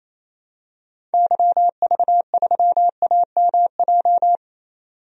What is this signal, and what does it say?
Morse code, heard as a keyed tone, about 28 wpm, 700 Hz. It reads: YV3AMJ
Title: DXCC-100 Call Signs - 6 Characters - 28wpm